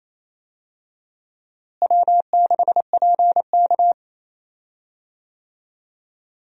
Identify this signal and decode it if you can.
Morse code, heard as a keyed tone, about 28 wpm, 700 Hz. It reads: W6PK
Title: USA Call Signs - 4 Characters - Repeated 2x - 28wpm